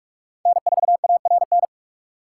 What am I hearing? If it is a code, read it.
Morse code, heard as a keyed tone, about 44 wpm, 700 Hz. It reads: N4ARN